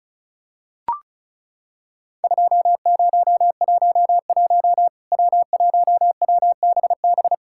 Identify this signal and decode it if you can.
Morse code, heard as a keyed tone, about 36 wpm, 700 Hz. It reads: 2011 W1WBB
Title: Worked All Europe DX Contest - QTC Exchange - 35wpm